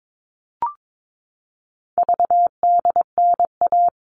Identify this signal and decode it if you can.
Morse code, heard as a keyed tone, about 22 wpm, 700 Hz. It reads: VDNA
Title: All Cut Numbers - 4-Digit 22wpm